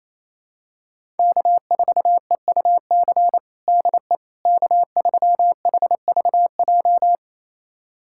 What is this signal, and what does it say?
Morse code, heard as a keyed tone, about 28 wpm, 700 Hz. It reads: K4EUC DE K3HVJ
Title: USA Callsign DE Callsign - 5 Characters - 28wpm